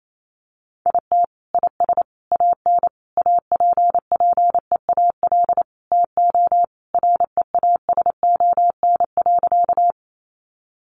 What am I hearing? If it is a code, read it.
Morse code, heard as a keyed tone, about 28 wpm, 700 Hz. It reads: IT IS AN APPEAL TO REASON.
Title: Sentences from Top 1300 Words 28wpm